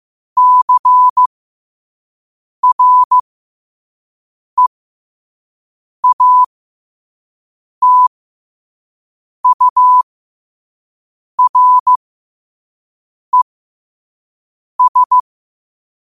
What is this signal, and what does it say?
Morse code, heard as a keyed tone, about 15 wpm, spaced as for 5 wpm, 1000 Hz. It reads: CREATURES
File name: qotdfiles / 5wpm